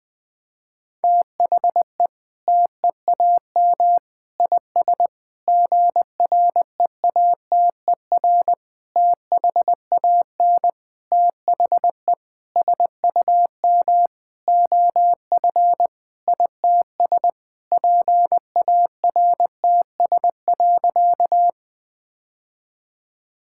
Morse code, heard as a keyed tone, about 19 wpm, 700 Hz. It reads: THE TEAM IS GREATER THAN THE SUM OF ITS PARTS.
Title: Sentences with 8-Word Phrases - 20wpm